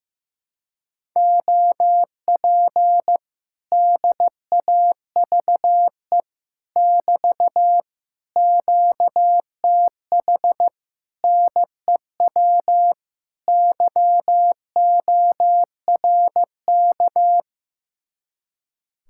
Morse code, heard as a keyed tone, about 15 wpm, 700 Hz. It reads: OP DAVE <BT> QTH NEW YORK
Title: Speed-Racing Common QSO Phrases Encore - 15wpm